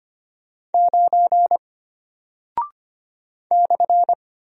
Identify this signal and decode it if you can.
Morse code, heard as a keyed tone, about 25 wpm, 700 Hz. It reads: 9 /
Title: Single Character (Letter:Number:Punctuation:Prosign) Alt 25wpm